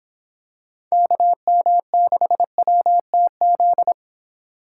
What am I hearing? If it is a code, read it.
Morse code, heard as a keyed tone, about 24 wpm, 700 Hz. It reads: KM6WTZ